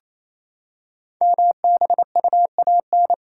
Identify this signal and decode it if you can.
Morse code, heard as a keyed tone, about 29 wpm, 700 Hz. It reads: MBUAN
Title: Code Groups - 5 Letters - 28wpm